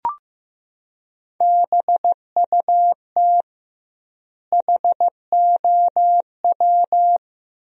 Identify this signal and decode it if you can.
Morse code, heard as a keyed tone, about 15 wpm, 700 Hz. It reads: BUT HOW
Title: 2x Word-Spacing - Sentences from Top 100 Words - 15wpm